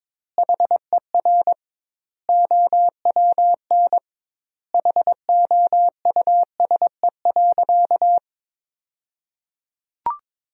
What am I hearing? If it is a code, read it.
Morse code, heard as a keyed tone, about 22 wpm, 700 Hz. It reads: HER OWN HOUSE.
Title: 2x Word-Spacing - Sentences from Top 300 Words - 22wpm